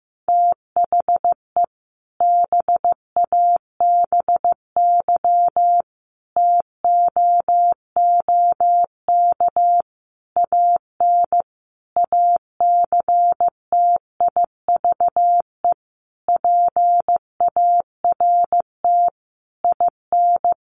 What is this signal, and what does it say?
Morse code, heard as a keyed tone, about 15 wpm, 700 Hz. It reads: THE BABY TOOK AN ACTIVE PART IN
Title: Sentences from Top 1300 Words 15wpm